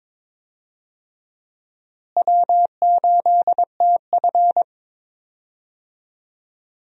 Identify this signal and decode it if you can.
Morse code, heard as a keyed tone, about 21 wpm, 700 Hz. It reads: W8TF
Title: USA Call Signs - 4 Characters - Repeated 3x - 22wpm